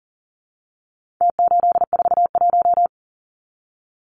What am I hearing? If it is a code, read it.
Morse code, heard as a keyed tone, about 40 wpm, 700 Hz. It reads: T841
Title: Numbers - Contest Style - 4-Digit 40wpm